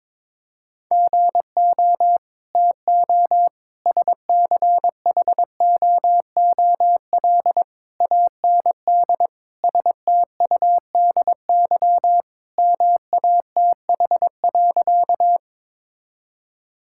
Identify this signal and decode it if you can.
Morse code, heard as a keyed tone, about 22 wpm, 700 Hz. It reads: GO TO SCHOOL AND STUDY MATH.
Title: Sentences from Top 1000 Words - Review Part 6 - 22wpm